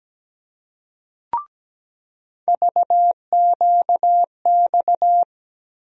Morse code, heard as a keyed tone, about 17 wpm, 700 Hz. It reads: VQX
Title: USA Call Sign Postfixes - 3 Letters 17wpm